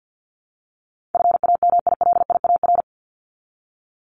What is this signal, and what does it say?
Morse code, heard as a keyed tone, about 48 wpm, 700 Hz. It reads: FAMILIAR
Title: Top 751-1000 Words 50wpm